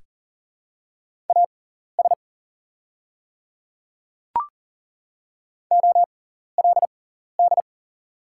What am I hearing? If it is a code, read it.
Morse code, heard as a keyed tone, about 40 wpm, spaced as for 14 wpm, 700 Hz. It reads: AS OLD
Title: 6x Character Spacing - Top 300 Words - 40wpm